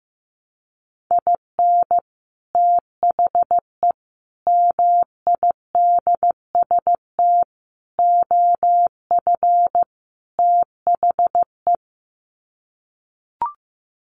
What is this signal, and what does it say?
Morse code, heard as a keyed tone, about 15 wpm, 700 Hz. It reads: IN THE MIDST OF THE
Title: Sets of 5 Words 15wpm